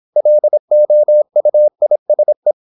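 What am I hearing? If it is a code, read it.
Morse code, heard as a keyed tone, about 26 wpm, 600 Hz. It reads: LOUISE
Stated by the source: High-speed (QRQ) Morse at 26 wpm.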